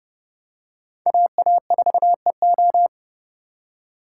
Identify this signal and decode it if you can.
Morse code, heard as a keyed tone, about 31 wpm, 700 Hz. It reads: AA4EO